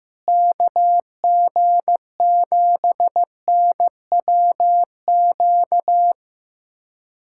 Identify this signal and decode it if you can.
Morse code, heard as a keyed tone, about 15 wpm, 700 Hz. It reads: KG7NWQ
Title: USA Callsign DE Callsign 15wpm